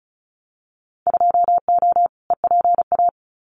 Morse code, heard as a keyed tone, about 34 wpm, 700 Hz. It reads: 2O EPA
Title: Winter Field Day Exchanges Encore 35wpm